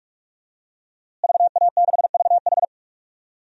Morse code, heard as a keyed tone, about 46 wpm, 700 Hz. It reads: VA6VH
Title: DXCC-100 Call Signs - 5 Characters - Repeated 3x - 45wpm